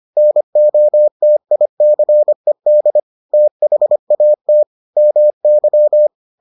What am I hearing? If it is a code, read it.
Morse code, heard as a keyed tone, about 25 wpm, 600 Hz. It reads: NOTICED THAT MY